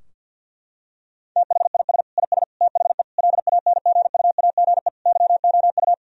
Morse code, heard as a keyed tone, about 50 wpm, 700 Hz. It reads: THIS IS THE LANGUAGE YOU
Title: Sentences from Top 1000 Words - Review Part 3 - 50wpm